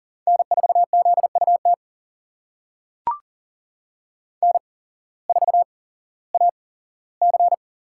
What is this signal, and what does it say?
Morse code, heard as a keyed tone, about 40 wpm, spaced as for 19 wpm, 700 Hz. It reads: N4ZUT N4AC
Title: USA Call Signs - 5 Characters - 8x Character Spacing - 40wpm